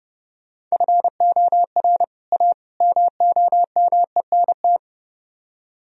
Morse code, heard as a keyed tone, about 38 wpm, 700 Hz. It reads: FOR A MOMENT
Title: Speed-Racing - Sets of 3 Words 30wpm